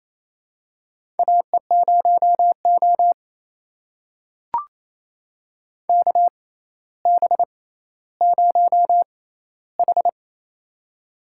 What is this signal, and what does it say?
Morse code, heard as a keyed tone, about 28 wpm, spaced as for 14 wpm, 700 Hz. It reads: AE0O KB0H
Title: USA Call Signs - 4 Characters - 6x Character Spacing - 28wpm